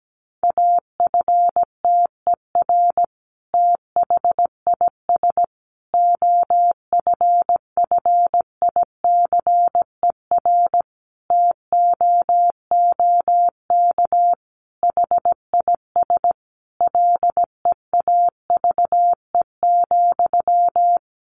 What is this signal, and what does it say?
Morse code, heard as a keyed tone, about 17 wpm, 700 Hz. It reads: AFTER THIS OFFICER TOOK HIS LEAVE,
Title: Sentences from Top 1200 Words 17wpm